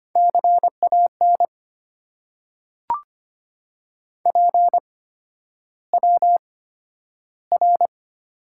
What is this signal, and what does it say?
Morse code, heard as a keyed tone, about 25 wpm, spaced as for 11 wpm, 690 Hz. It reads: CAN PWR
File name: Top 90 QSO Elements - 8x Character Spacing - 25wpm